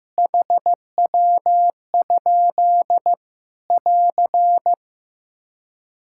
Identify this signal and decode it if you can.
Morse code, heard as a keyed tone, about 15 wpm, 700 Hz. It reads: HW? <AR>
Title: Practice QSO - Exchange 3 - More Info - 15wpm